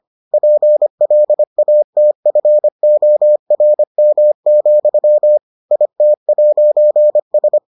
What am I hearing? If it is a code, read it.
Morse code, heard as a keyed tone, about 25 wpm, 600 Hz. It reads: PLATFORM, IT'S